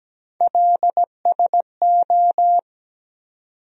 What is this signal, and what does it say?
Morse code, heard as a keyed tone, about 17 wpm, 700 Hz. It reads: LSO
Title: USA Call Sign Postfixes - 3 Letters 17wpm